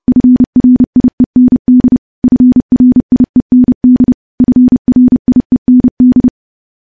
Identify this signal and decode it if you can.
Morse code, heard as a keyed tone, about 28 wpm, 250 Hz. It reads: FRIEND FRIEND FRIEND
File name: morse / morse-training-30WPM